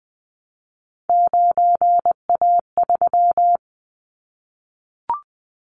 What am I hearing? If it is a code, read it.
Morse code, heard as a keyed tone, about 20 wpm, 700 Hz. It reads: 9A3
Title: Callsign Prefixes for Top 50 DXCC - 20wpm